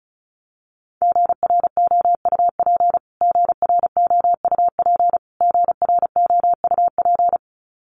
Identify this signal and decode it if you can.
Morse code, heard as a keyed tone, about 34 wpm, 700 Hz. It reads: GROUP GROUP GROUP
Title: Top 500 Words Repeated 35wpm